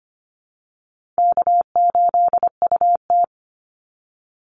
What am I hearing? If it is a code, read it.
Morse code, heard as a keyed tone, about 25 wpm, 700 Hz. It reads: K8UT
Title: Call Signs Repeated 3x - 25wpm